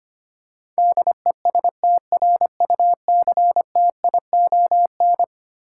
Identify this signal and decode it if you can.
Morse code, heard as a keyed tone, about 25 wpm, 700 Hz. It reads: DESTRUCTION
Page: morse-code-ninja-practice-25wpm